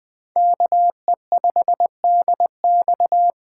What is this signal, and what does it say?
Morse code, heard as a keyed tone, about 20 wpm, 700 Hz. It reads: KE5DX
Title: Speed Racing - USA 2x2 Call Signs 17wpm